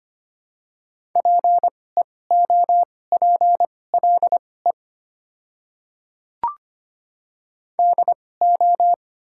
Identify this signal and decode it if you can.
Morse code, heard as a keyed tone, about 25 wpm, spaced as for 18 wpm, 700 Hz. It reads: PEOPLE DO
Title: 2x Character Spacing - Top 200 Words - 25wpm